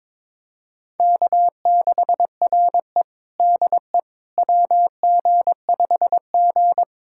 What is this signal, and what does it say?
Morse code, heard as a keyed tone, about 22 wpm, 700 Hz. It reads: K6RE DE WG5G
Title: USA Callsign DE Callsign - 4 Characters - 22wpm